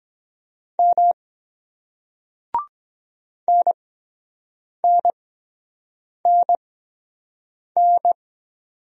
Morse code, heard as a keyed tone, about 23 wpm, spaced as for 7 wpm, 700 Hz. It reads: M NNNN